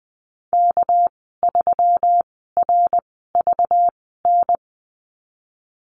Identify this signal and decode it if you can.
Morse code, heard as a keyed tone, about 20 wpm, spaced as for 15 wpm, 700 Hz. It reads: K3RVN
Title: 213 Lesson 34g - Number 8 - Call Signs 20wpm